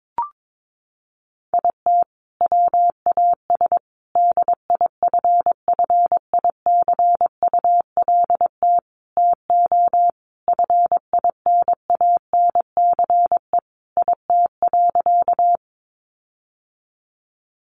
Morse code, heard as a keyed tone, about 22 wpm, 700 Hz. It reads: IT WAS DIFFICULT TO FINANCE IT.
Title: Sentences from Top 1500 Words 22wpm